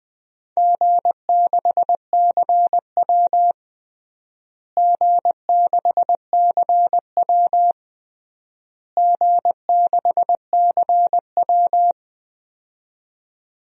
Morse code, heard as a keyed tone, about 20 wpm, 700 Hz. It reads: G6CW G6CW G6CW
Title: DXCC-100 Call Signs - 4 Characters - Repeated 3x - 20wpm